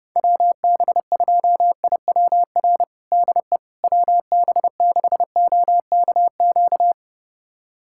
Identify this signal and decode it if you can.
Morse code, heard as a keyed tone, about 30 wpm, 700 Hz. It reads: WB2IWR DE WB6OKQ
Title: USA Callsign DE Callsign 30wpm